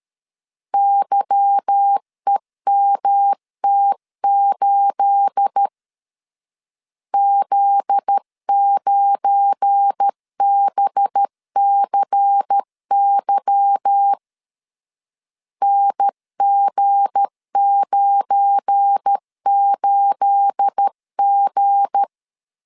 Morse code, heard as a keyed tone, about 13 wpm, 800 Hz. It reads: YEMT8 Z9BCY NG98G
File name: LEZIONI / 9 28 12wpm 2351